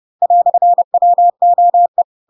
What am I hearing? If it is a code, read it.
Morse code, heard as a keyed tone, about 30 wpm, 700 Hz. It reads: "WOE
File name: Pinocchio / Pinocchio ch4 30 30 ik1hge